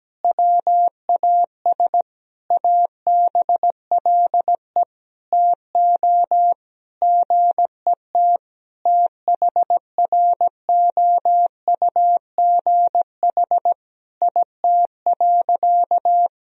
Morse code, heard as a keyed tone, about 17 wpm, 700 Hz. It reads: WAS ABLE TO GET THROUGH IT.
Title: Sentences from Top 1000 Words - Review Part 2 - 17wpm